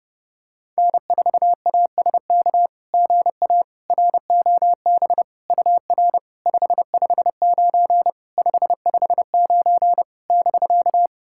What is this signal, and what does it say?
Morse code, heard as a keyed tone, about 32 wpm, 690 Hz. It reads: N4ASK GA ROB UR 559 559 <BK>
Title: SOTA QSO Practice 30wpm